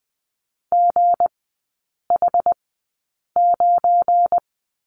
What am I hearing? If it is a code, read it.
Morse code, heard as a keyed tone, about 20 wpm, spaced as for 8 wpm, 700 Hz. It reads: GH9
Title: 172 Lesson 28e - Number 9 - Three Characters 20wpm